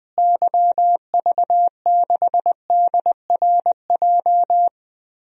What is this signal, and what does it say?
Morse code, heard as a keyed tone, about 20 wpm, 700 Hz. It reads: YV6DRJ
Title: DXCC-100 Call Signs - 6 Characters - Repeated 2x - 20wpm